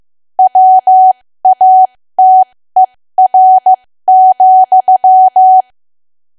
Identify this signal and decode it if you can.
Morse code, heard as a keyed tone, about 15 wpm, spaced as for 13 wpm, 750 Hz. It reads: WATER,
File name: cwa / W1AW-Level2 / 13WPM